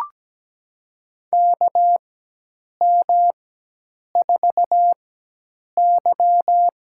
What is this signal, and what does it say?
Morse code, heard as a keyed tone, about 17 wpm, spaced as for 8 wpm, 700 Hz. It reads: KM4Y
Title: USA Call Signs - 4x Character Spacing - 17wpm